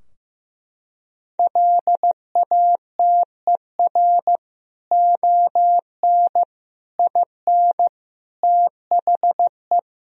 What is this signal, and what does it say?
Morse code, heard as a keyed tone, about 15 wpm, 700 Hz. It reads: LATER ON IN THE
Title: Sentences from Top 1000 Words - Review Part 4 - 15wpm